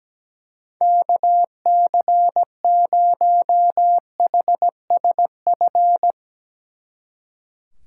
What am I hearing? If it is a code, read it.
Morse code, heard as a keyed tone, about 18 wpm, 700 Hz. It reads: KC0HSF
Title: Speed Racing - USA Call Signs 17wpm